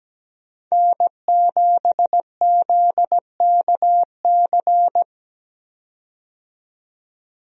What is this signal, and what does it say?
Morse code, heard as a keyed tone, about 17 wpm, 700 Hz. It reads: N7ZKC
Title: USA Call Signs - 5 Characters - 17wpm